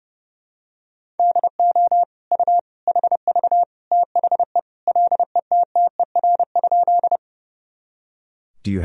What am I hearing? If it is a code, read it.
Morse code, heard as a keyed tone, about 29 wpm, 700 Hz. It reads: DO U HV THE LETTER?
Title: Sentences from Top 300 Words and Abbreviations 30wpm